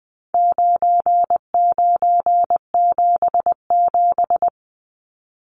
Morse code, heard as a keyed tone, about 19 wpm, 700 Hz. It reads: 9977
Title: Numbers - 4-Digit 20wpm